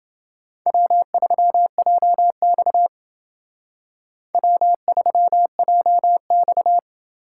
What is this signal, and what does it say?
Morse code, heard as a keyed tone, about 28 wpm, 700 Hz. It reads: W3JX W3JX